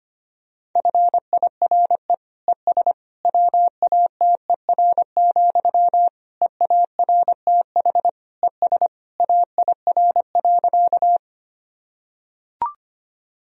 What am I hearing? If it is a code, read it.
Morse code, heard as a keyed tone, about 25 wpm, 700 Hz. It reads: FIRE ES WATER, EARTH ES AIR.
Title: Sentences from Top 400 Words and Abbreviations 25wpm